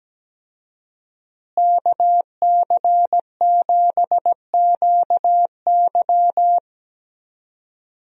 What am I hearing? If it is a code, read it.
Morse code, heard as a keyed tone, about 17 wpm, 700 Hz. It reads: KC7QY
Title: USA Call Signs - 5 Characters - Repeated 2x - 17wpm